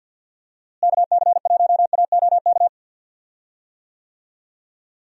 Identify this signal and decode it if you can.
Morse code, heard as a keyed tone, about 50 wpm, 700 Hz. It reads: KC1AOK
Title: USA Call Signs - 6 Characters - Repeated 2x - 50wpm